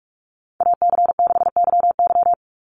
Morse code, heard as a keyed tone, about 44 wpm, 700 Hz. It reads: AC6YY
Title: Call Signs - 45wpm